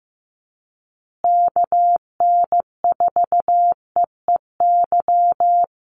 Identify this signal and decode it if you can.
Morse code, heard as a keyed tone, about 15 wpm, 700 Hz. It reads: KN4EEY
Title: USA Call Signs Repeated 3x 15wpm